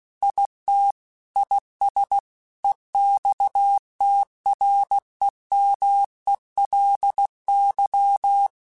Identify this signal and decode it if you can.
Morse code, heard as a keyed tone, about 16 wpm, 800 Hz. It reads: IT IS EXTREMELY
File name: Technical Subjects / Technical Subjects-mp3s / AND Gate @15